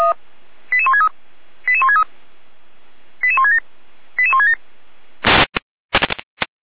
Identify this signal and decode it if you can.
Five-tone selective call (ZVEI), heard as quick runs of five tones, tones of about 70 ms. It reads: 88163 88163 88167 88167
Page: SDR/ZVEI